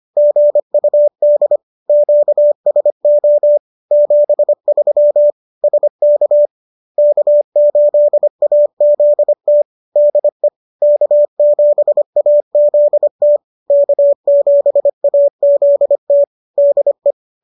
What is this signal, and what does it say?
Morse code, heard as a keyed tone, about 25 wpm, 590 Hz. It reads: GUD QSO 73 SK K8AZT DE K7AZT K7AZT DE